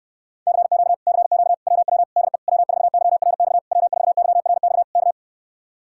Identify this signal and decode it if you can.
Morse code, heard as a keyed tone, about 69 wpm, 700 Hz. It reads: CQ CQ WW DE W4YRX W4YRX K